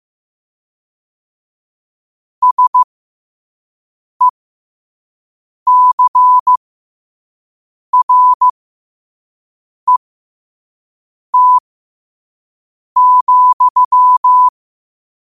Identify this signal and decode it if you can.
Morse code, heard as a keyed tone, about 15 wpm, spaced as for 5 wpm, 1000 Hz. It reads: SECRET,
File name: qotdfiles / 5wpm